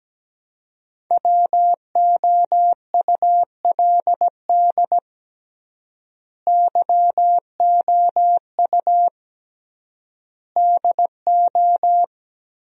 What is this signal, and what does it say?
Morse code, heard as a keyed tone, about 17 wpm, 700 Hz. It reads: WOULD YOU DO
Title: 3x Word-Spacing - Sentences from Top 100 Words - 17wpm